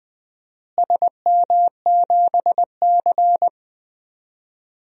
Morse code, heard as a keyed tone, about 20 wpm, 700 Hz. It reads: SM7C
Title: DXCC-100 Call Signs - 4 Characters - Repeated 3x - 20wpm